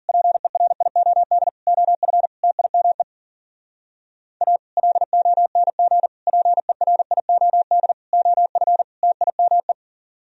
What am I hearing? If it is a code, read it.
Morse code, heard as a keyed tone, about 52 wpm, 700 Hz. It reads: PERIOD OF TIME A LONG PERIOD OF TIME